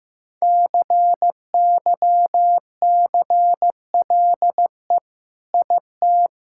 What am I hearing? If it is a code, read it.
Morse code, heard as a keyed tone, about 16 wpm, 700 Hz. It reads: CYCLE IT
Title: Sentences from Top 1000 Words - Review Part 6 - 15wpm